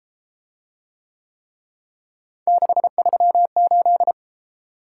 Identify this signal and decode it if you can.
Morse code, heard as a keyed tone, about 32 wpm, 700 Hz. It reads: 638